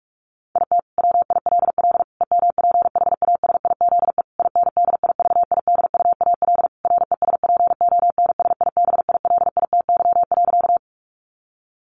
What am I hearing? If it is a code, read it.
Morse code, heard as a keyed tone, about 44 wpm, 700 Hz. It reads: IT WILL EMPHASIZE INDIVIDUAL RESPONSIBILITY.
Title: Sentences from Top 1500 Words 45wpm